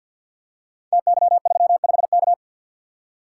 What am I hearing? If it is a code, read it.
Morse code, heard as a keyed tone, about 47 wpm, 700 Hz. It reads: TY35K
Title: Code Groups - 5 Letter:Numbers - 50wpm